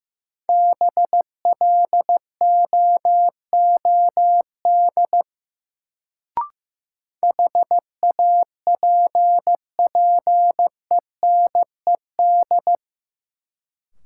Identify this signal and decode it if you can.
Morse code, heard as a keyed tone, about 16 wpm, 700 Hz. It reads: BLOOD HAPPENED
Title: Sentences from Top 1000 Words - Review Part 2 - New Word Drill - 15wpm